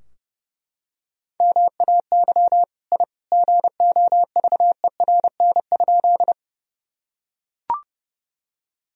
Morse code, heard as a keyed tone, about 30 wpm, 700 Hz. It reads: MAY I GOVERN?
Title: Sentences from Top 1000 Words - Review Part 2 - 30wpm